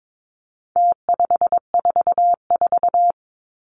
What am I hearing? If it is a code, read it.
Morse code, heard as a keyed tone, about 22 wpm, 700 Hz. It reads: T544
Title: Numbers - Contest Style - 4-Digit 22wpm